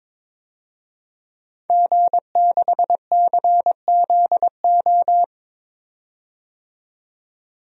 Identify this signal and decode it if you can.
Morse code, heard as a keyed tone, about 22 wpm, 700 Hz. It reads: G6CZO
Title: DXCC-100 Call Signs - 5 Characters - 22wpm